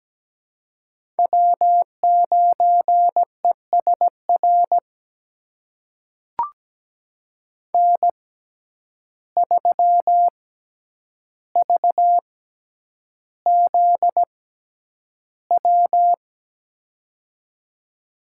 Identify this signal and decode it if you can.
Morse code, heard as a keyed tone, about 17 wpm, spaced as for 9 wpm, 700 Hz. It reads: W9ESR N3VZW
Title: USA Call Signs - 5 Characters - 6x Character Spacing - 17wpm